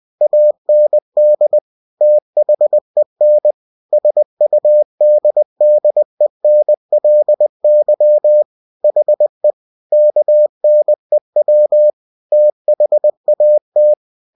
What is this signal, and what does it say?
Morse code, heard as a keyed tone, about 20 wpm, 600 Hz. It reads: AND THEN SUDDENLY HE KNEW THAT
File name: cw-books / winnie-the-pooh / winnie-20wpm-page-0093